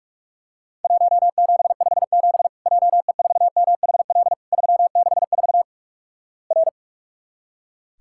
Morse code, heard as a keyed tone, about 46 wpm, 700 Hz. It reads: 1757 JE4MHL 364 R
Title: Worked All Europe DX Contest - QTC Exchange - 45wpm